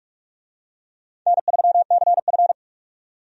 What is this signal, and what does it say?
Morse code, heard as a keyed tone, about 43 wpm, 700 Hz. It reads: N3CF
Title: USA Call Signs - 4 Characters - 45wpm